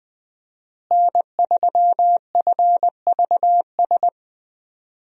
Morse code, heard as a keyed tone, about 20 wpm, 700 Hz. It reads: N3FVS
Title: USA Call Signs - 5 Characters - Repeated 2x - 20wpm